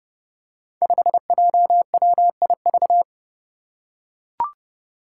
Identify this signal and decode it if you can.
Morse code, heard as a keyed tone, about 30 wpm, 700 Hz. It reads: 5JWIV
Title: Five Challenging Letter-Number Combinations 30wpm